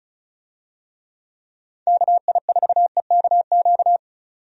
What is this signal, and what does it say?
Morse code, heard as a keyed tone, about 34 wpm, 700 Hz. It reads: KI4EKQ
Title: USA Call Signs - 6 Characters - 35wpm